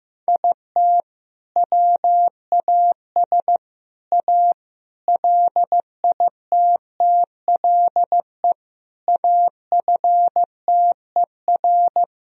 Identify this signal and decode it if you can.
Morse code, heard as a keyed tone, about 15 wpm, 700 Hz. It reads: IT WAS A LITTLE AFTER
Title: Sentences from Top 1000 Words - Review Part 3 - 15wpm